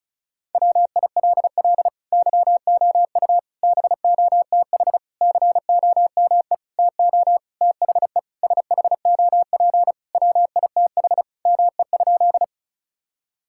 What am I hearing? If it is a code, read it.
Morse code, heard as a keyed tone, about 36 wpm, 700 Hz. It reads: WILL YOU BOTH COME TO THE SHOP WITH ME?
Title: Sentences from Top 1600 Words 35wpm